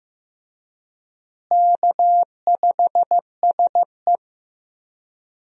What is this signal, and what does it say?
Morse code, heard as a keyed tone, about 15 wpm, 700 Hz. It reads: K5SE
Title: USA Call Signs - 4 Characters - Repeated 3x - 15wpm